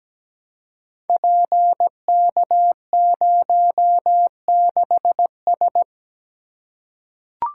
Morse code, heard as a keyed tone, about 17 wpm, 700 Hz. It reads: PK06S
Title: Code Groups - 5 Letter:Numbers - 17wpm